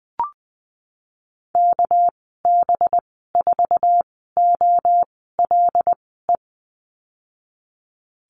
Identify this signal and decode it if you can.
Morse code, heard as a keyed tone, about 20 wpm, spaced as for 15 wpm, 700 Hz. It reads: KB4OLE
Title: 231 Lesson 37g - Letter x - Call Signs 20wpm